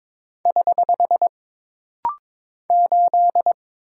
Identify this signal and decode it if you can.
Morse code, heard as a keyed tone, about 22 wpm, 700 Hz. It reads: <HH> 8
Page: This rapid-fire timing is geared to push your ICR (Instant Character Recognition) skills.